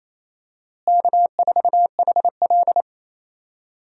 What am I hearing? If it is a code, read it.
Morse code, heard as a keyed tone, about 28 wpm, 700 Hz. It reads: K4HL
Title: USA Call Signs - 4 Characters - Repeated 3x - 28wpm